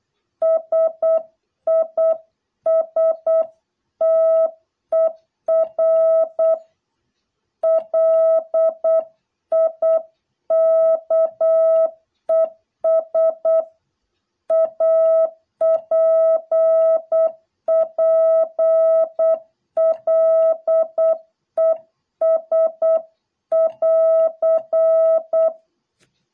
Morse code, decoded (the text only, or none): SISTER LIKES APPLES<AR>